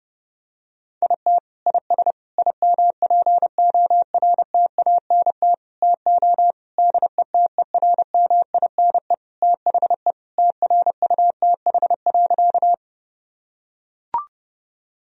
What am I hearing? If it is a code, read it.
Morse code, heard as a keyed tone, about 30 wpm, 700 Hz. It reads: IT IS IMPORTANT TO DETERMINE THE TRUTH.
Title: Sentences from Top 1000 Words - Review Part 4 - 30wpm